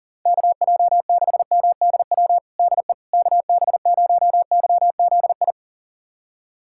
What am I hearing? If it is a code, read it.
Morse code, heard as a keyed tone, about 40 wpm, 700 Hz. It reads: KJ6MDW DE KB0YZI
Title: USA Callsign DE Callsign 40wpm